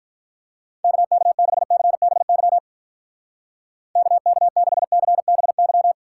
Morse code, heard as a keyed tone, about 42 wpm, 700 Hz. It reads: KK6CBY KK6CBY
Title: Speed Racing - USA 2x3 Call Signs 35wpm